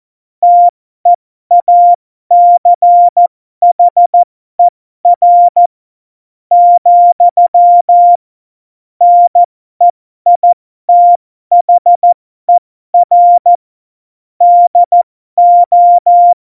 Morse code, heard as a keyed tone, about 14 wpm, spaced as for 12 wpm, 700 Hz. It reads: TEACHER, NEITHER DO
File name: Morsekurs / Witze